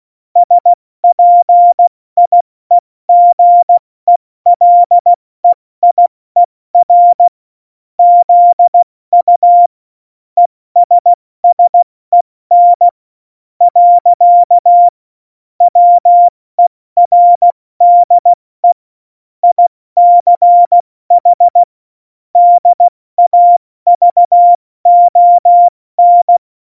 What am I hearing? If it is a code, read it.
Morse code, heard as a keyed tone, about 16 wpm, 700 Hz. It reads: SPIEGELEIER ZU ESSEN . WERDE ICH DAVON
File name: Morsekurs / Witze / Witz 02. Januar 2018, 70 F80